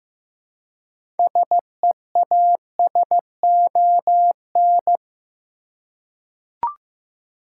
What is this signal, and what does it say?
Morse code, heard as a keyed tone, about 15 wpm, 700 Hz. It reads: SEASON
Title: Six Letter Words - 15wpm